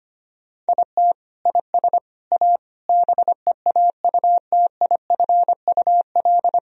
Morse code, heard as a keyed tone, about 25 wpm, 700 Hz. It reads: IT IS A BEAUTIFUL
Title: Sentences from Top 1000 Words - Review Part 2 - 25wpm